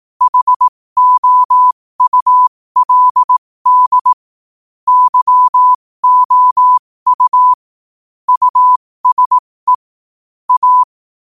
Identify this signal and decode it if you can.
Morse code, heard as a keyed tone, about 18 wpm, spaced as for 15 wpm, 1000 Hz. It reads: HOULD YOU USE A
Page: New quotes every day in morse code at 15 Words per minute.